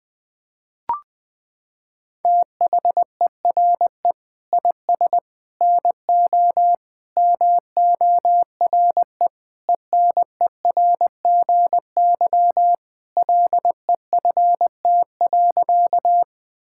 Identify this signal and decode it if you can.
Morse code, heard as a keyed tone, about 21 wpm, 700 Hz. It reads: THERE IS NO MORE ENERGY LEFT.
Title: Sentences from Top 1000 Words - Review Part 6 - 20wpm